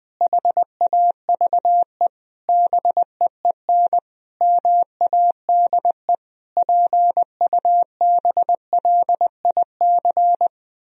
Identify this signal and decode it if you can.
Morse code, heard as a keyed tone, about 20 wpm, 700 Hz. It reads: HAVE BEEN MADE PUBLIC
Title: Sentences from Top 1000 Words - Review Part 3 - 20wpm